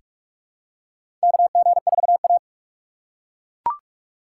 Morse code, heard as a keyed tone, about 43 wpm, 700 Hz. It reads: KG4A
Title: USA Call Signs - 4 Characters - 4x Character Spacing - 45wpm